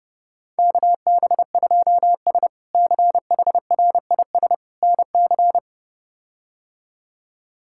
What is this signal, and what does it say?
Morse code, heard as a keyed tone, about 30 wpm, 700 Hz. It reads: KB2S CHRIS NC
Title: K1USN SST QSO Practice - 30wpm